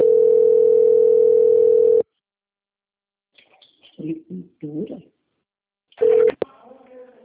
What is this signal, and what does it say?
Ringback tone (North America), 440 + 480 Hz, with speech.
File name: out- - - - - .16160